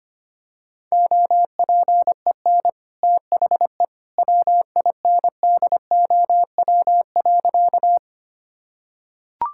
Morse code, heard as a keyed tone, about 25 wpm, 700 Hz. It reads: OPEN THE WINDOW.
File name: Easy Sentences with Hints - 25wpm